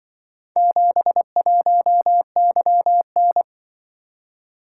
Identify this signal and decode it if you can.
Morse code, heard as a keyed tone, about 24 wpm, 700 Hz. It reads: 71YN
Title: Speed-Racing - Four Letter-Number Combinations - 20wpm